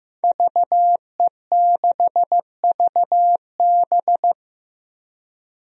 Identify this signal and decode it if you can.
Morse code, heard as a keyed tone, about 16 wpm, 700 Hz. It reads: VE6VB
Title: DXCC-100 Call Signs - 5 Characters - Repeated 3x - 15wpm